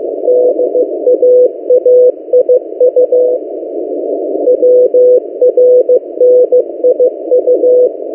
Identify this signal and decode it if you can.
Morse code, heard as a keyed tone, about 15 wpm, 500 Hz. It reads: DAAIU WRNIU
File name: M08a- -1355utc-8096khz